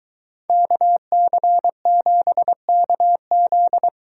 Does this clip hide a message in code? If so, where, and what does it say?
Yes: Morse code from 0.5 to 3.9 s, heard as a keyed tone, about 23 wpm, 700 Hz. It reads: KC7KZ